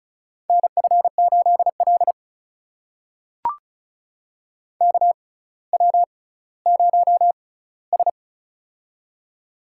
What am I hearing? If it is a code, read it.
Morse code, heard as a keyed tone, about 36 wpm, spaced as for 18 wpm, 700 Hz. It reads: NF8L KW0S
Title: USA Call Signs - 6x Character Spacing - 35wpm